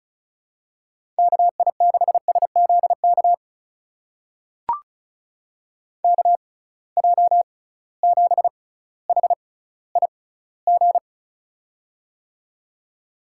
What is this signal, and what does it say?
Morse code, heard as a keyed tone, about 34 wpm, spaced as for 18 wpm, 700 Hz. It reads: KI6SZK KJ7HIG
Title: USA Call Signs - 6 Characters - 6x Character Spacing - 35wpm